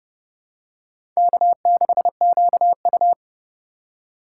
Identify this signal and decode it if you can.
Morse code, heard as a keyed tone, about 30 wpm, 700 Hz. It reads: K6QU
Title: USA Call Signs - 4 Characters - 2x Character Spacing - 30wpm